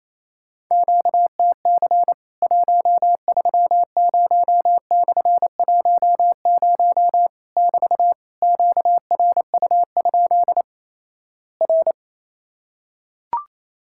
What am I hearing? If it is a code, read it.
Morse code, heard as a keyed tone, about 28 wpm, 700 Hz. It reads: QTC 130/10 <BT> QRU? R
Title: Worked All Europe DX Contest - QTC Exchange - 28wpm